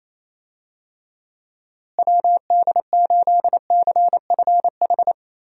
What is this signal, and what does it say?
Morse code, heard as a keyed tone, about 28 wpm, 700 Hz. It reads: WD8CFH
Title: USA Call Signs - 6 Characters - Repeated 2x - 28wpm